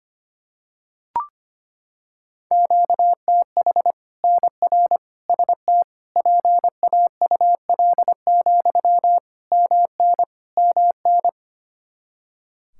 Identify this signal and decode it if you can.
Morse code, heard as a keyed tone, about 25 wpm, 700 Hz. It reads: QTH NR ST PAUL, MN MN
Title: Practice QSO - Exchange 1 - Beginning - 25wpm